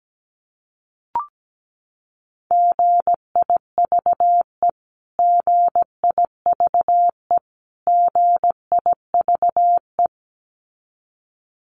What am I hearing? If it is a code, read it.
Morse code, heard as a keyed tone, about 17 wpm, 700 Hz. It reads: GIVE GIVE GIVE
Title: Top 500 Words Repeated 17wpm